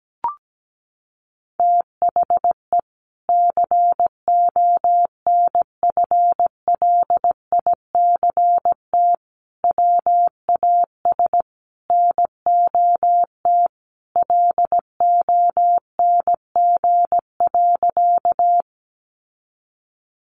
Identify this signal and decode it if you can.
Morse code, heard as a keyed tone, about 17 wpm, 700 Hz. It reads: THE CONFLICT WAS NOT LONG.
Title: Sentences from Top 1300 Words 17wpm